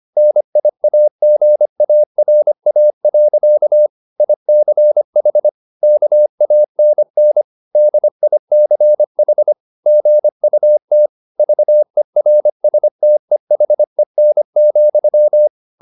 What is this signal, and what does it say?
Morse code, heard as a keyed tone, about 25 wpm, 600 Hz. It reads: NIAGARA. ICH KANN DICH GUT VERSTEHEN,